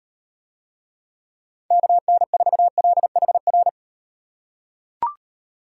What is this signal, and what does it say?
Morse code, heard as a keyed tone, about 38 wpm, 700 Hz. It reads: KN4LHR